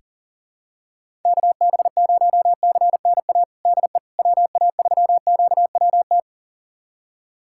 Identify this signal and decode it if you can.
Morse code, heard as a keyed tone, about 40 wpm, 700 Hz. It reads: KB0CNA DE WA3QWT
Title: USA Callsign DE Callsign - 6 Characters - 40wpm